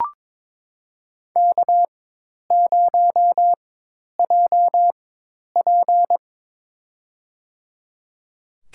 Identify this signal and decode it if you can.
Morse code, heard as a keyed tone, about 22 wpm, spaced as for 10 wpm, 700 Hz. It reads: K0JP
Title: USA Call Signs - 4 Characters - 4x Character Spacing - 22wpm